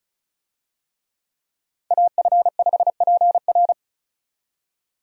Morse code, heard as a keyed tone, about 36 wpm, 700 Hz. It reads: AF5PR
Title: USA Call Signs - 5 Characters - 35wpm